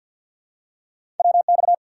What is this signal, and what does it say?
Morse code, heard as a keyed tone, about 47 wpm, 700 Hz. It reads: WX
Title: Top 90 QSO Elements 50wpm